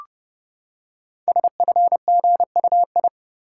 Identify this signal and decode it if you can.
Morse code, heard as a keyed tone, about 29 wpm, 700 Hz. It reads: SFGUI
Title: Code Groups - 5 Letter:Numbers - 30wpm